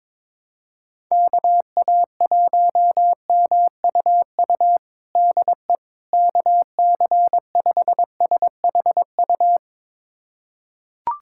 Morse code, heard as a keyed tone, about 22 wpm, 700 Hz. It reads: KA1MUU DE KC5SHU
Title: USA Callsign DE Callsign - 6 Characters - 22wpm